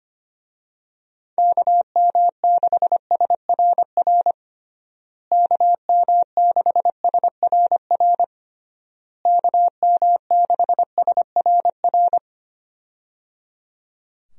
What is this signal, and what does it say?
Morse code, heard as a keyed tone, about 25 wpm, 700 Hz. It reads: KM6SRR KM6SRR KM6SRR
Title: USA Call Signs - 6 Characters - Repeated 3x - 25wpm